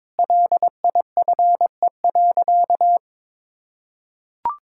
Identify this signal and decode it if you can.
Morse code, heard as a keyed tone, about 22 wpm, 700 Hz. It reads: LIFE.
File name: practice / 22wpm / Sentences with 7-Word Phrases - 22wpm